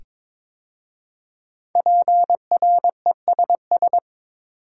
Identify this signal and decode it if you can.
Morse code, heard as a keyed tone, about 22 wpm, 700 Hz. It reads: PRESS
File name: 5 Letter Words - First 3 Letters Given - 22wpm